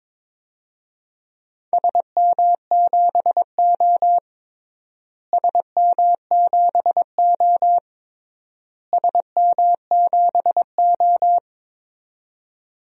Morse code, heard as a keyed tone, about 22 wpm, 700 Hz. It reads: SM7O SM7O SM7O
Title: DXCC-100 Call Signs - 4 Characters - Repeated 3x - 22wpm